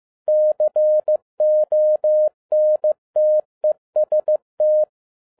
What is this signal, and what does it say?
Morse code, heard as a keyed tone, about 15 wpm, 610 Hz. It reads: CONTEST
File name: 2011-10-mp3 / n02-FIELD DAY CONTEST QRV FROM SAKAE VILLAGE 15wpm